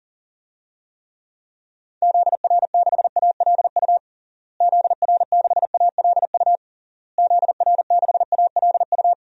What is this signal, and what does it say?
Morse code, heard as a keyed tone, about 40 wpm, 700 Hz. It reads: ZR6ALU ZR6ALU ZR6ALU
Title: DXCC-100 Call Signs - 6 Characters - Repeated 3x - 40wpm